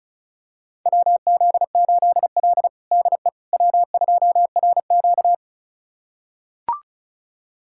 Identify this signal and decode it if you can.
Morse code, heard as a keyed tone, about 36 wpm, 700 Hz. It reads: WZ8L DE W2RQ
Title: USA Callsign DE Callsign - 4 Characters - 35wpm